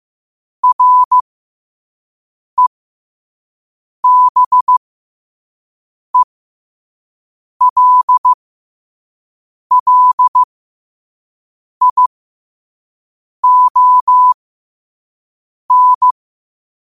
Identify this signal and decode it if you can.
Morse code, heard as a keyed tone, about 15 wpm, spaced as for 5 wpm, 1000 Hz. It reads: REBELLION
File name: qotdfiles / 5wpm